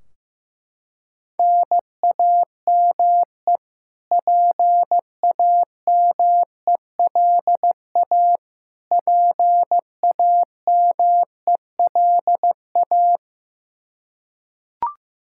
Morse code, heard as a keyed tone, about 15 wpm, 700 Hz. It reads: NAME PAMELA PAMELA
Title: Practice QSO - Exchange 1 - Beginning - 15wpm